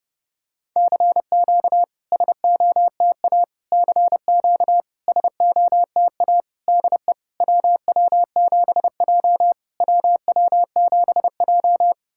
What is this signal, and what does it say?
Morse code, heard as a keyed tone, about 30 wpm, 690 Hz. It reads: CQ SOTA CQ SOTA DE WW7J WW7J
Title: SOTA QSO Practice 30wpm